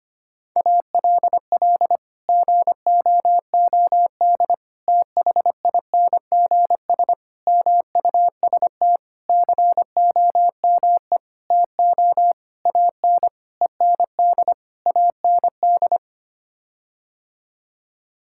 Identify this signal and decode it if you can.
Morse code, heard as a keyed tone, about 25 wpm, 700 Hz. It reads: ALL GOOD THINGS MUST COME TO AN END AND
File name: Sets of 9 Words - 25wpm